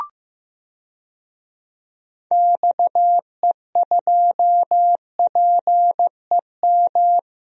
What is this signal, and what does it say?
Morse code, heard as a keyed tone, about 15 wpm, 700 Hz. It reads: XE2PEM
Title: DXCC-100 Call Signs - 6 Characters - Repeated 3x - 15wpm